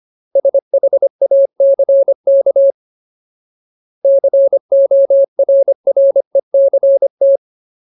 Morse code, heard as a keyed tone, about 25 wpm, 550 Hz. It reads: SHACK CORRECT
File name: Ultimate Word List / 25WPM Ultimate